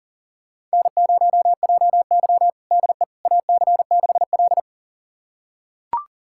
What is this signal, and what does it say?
Morse code, heard as a keyed tone, about 40 wpm, 700 Hz. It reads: N0JY DE AC6L
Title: USA Callsign DE Callsign - 4 Characters - 40wpm